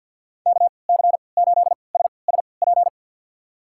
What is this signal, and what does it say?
Morse code, heard as a keyed tone, about 50 wpm, spaced as for 28 wpm, 700 Hz. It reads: KC8SSP
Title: USA Call Signs - 6 Characters - 3x Character Spacing - 50wpm